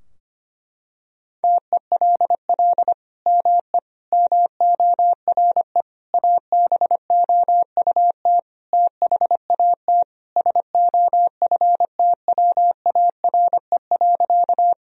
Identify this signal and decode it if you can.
Morse code, heard as a keyed tone, about 25 wpm, 700 Hz. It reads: TELL ME MORE ABOUT THAT SOFTWARE.
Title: Sentences from Top 1000 Words - Review Part 3 - 25wpm